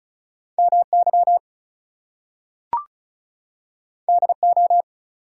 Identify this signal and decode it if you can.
Morse code, heard as a keyed tone, about 36 wpm, 700 Hz. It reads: MY DO
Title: Two Letter Words - 35wpm